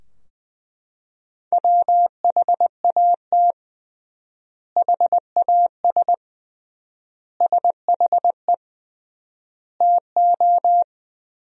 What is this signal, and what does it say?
Morse code, heard as a keyed tone, about 20 wpm, 700 Hz. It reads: WHAT HAS SHE TO
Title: 3x Word-Spacing - Sentences from Top 100 Words - 20wpm